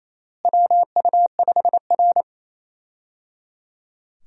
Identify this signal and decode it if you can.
Morse code, heard as a keyed tone, about 28 wpm, 700 Hz. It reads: WU5R
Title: USA Call Signs - 4 Characters - Repeated 3x - 28wpm